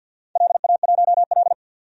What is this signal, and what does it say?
Morse code, heard as a keyed tone, about 51 wpm, 700 Hz. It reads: LA1L
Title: DXCC-100 Call Signs - 4 Characters - 50wpm